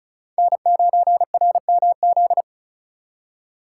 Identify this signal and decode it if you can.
Morse code, heard as a keyed tone, about 45 wpm, 700 Hz. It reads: N9RMZ